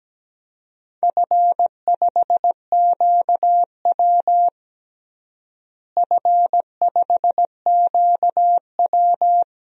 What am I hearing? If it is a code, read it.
Morse code, heard as a keyed tone, about 17 wpm, 700 Hz. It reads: F5QW F5QW
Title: DXCC-100 Call Signs - 4 Characters - Repeated 3x - 17wpm